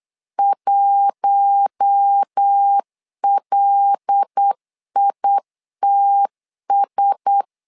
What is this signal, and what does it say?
Morse code, heard as a keyed tone, about 8 wpm, 800 Hz. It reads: 1LITS
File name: LEZIONI / 5 LBVC / 5 L 8wpm 2831